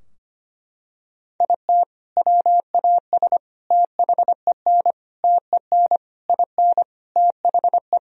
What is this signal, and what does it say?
Morse code, heard as a keyed tone, about 25 wpm, 700 Hz. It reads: IT WAS THEN TEN IN THE
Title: Sentences from Top 1000 Words - Review Part 2 - 25wpm